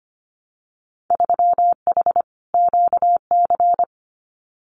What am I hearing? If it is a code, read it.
Morse code, heard as a keyed tone, about 25 wpm, 700 Hz. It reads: 3H QC
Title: Winter Field Day Exchanges 25wpm